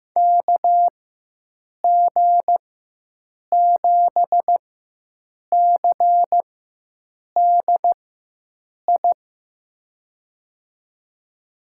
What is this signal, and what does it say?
Morse code, heard as a keyed tone, about 15 wpm, spaced as for 7 wpm, 700 Hz. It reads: KG7CDI
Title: USA Call Signs - 6 Characters - 4x Character Spacing - 15wpm